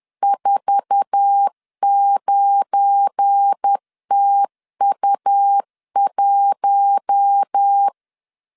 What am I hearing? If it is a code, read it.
Morse code, heard as a keyed tone, about 10 wpm, 800 Hz. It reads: 49TU1
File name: LEZIONI / 8 YJ45 / 8 YJ45 10wpm 2549